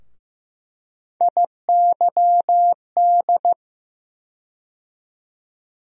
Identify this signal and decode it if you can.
Morse code, heard as a keyed tone, about 15 wpm, 700 Hz. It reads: IYD
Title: USA Call Sign Postfixes - 3 Letters 15wpm